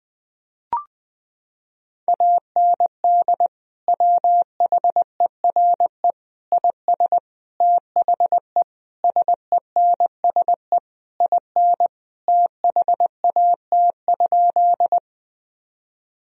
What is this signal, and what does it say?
Morse code, heard as a keyed tone, about 20 wpm, 700 Hz. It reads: AND WHERE IS THE SENSE IN THAT?
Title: Sentences from Top 1000 Words - Review Part 4 - 20wpm